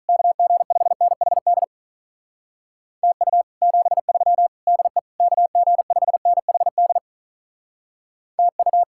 Morse code, heard as a keyed tone, about 45 wpm, 700 Hz. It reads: KG5NHD TU 73 DE KG5NHD TU